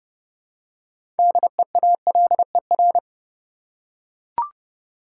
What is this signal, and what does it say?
Morse code, heard as a keyed tone, about 30 wpm, 700 Hz. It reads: DEALER
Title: Six Letter Words - 30wpm